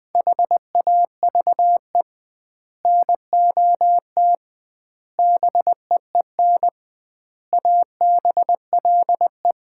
Morse code, heard as a keyed tone, about 20 wpm, 700 Hz. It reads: HAVE NOT BEEN ABLE
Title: 2x Word-Spacing - Sets of 5 Words - 20wpm